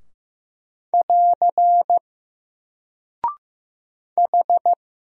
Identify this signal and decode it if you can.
Morse code, heard as a keyed tone, about 15 wpm, 700 Hz. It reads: <AR> H
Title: Single Character (Letter:Number:Punctuation:Prosign) 15wpm